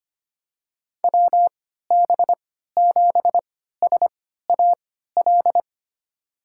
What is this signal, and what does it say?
Morse code, heard as a keyed tone, about 25 wpm, spaced as for 14 wpm, 700 Hz. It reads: WB7SAL
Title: USA Call Signs - 3x Character Spacing - 25wpm